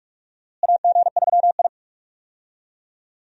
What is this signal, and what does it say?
Morse code, heard as a keyed tone, about 42 wpm, 700 Hz. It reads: AG3I
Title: USA Call Signs - 4 Characters - Repeated 2x - 45wpm